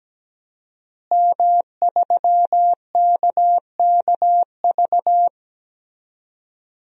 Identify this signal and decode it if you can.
Morse code, heard as a keyed tone, about 17 wpm, 700 Hz. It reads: M3KKV
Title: DXCC-100 Call Signs - 5 Characters - Repeated 3x - 17wpm